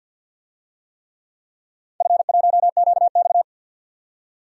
Morse code, heard as a keyed tone, about 51 wpm, 700 Hz. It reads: F1QX
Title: DXCC-100 Call Signs - 4 Characters - 50wpm